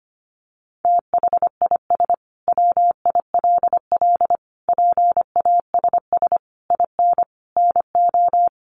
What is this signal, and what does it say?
Morse code, heard as a keyed tone, about 25 wpm, 700 Hz. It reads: THIS WILL PASS IN NO
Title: Sentences from Top 400 Words 25wpm